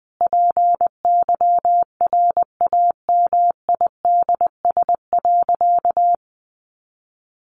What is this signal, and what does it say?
Morse code, heard as a keyed tone, about 20 wpm, 700 Hz. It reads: PYRAMIDS.
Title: A Sound Of Thunder 20wpm